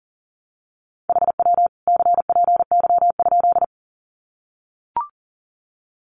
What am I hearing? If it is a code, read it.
Morse code, heard as a keyed tone, about 38 wpm, 700 Hz. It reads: HW CPY?
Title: Common QSO Phrases 40wpm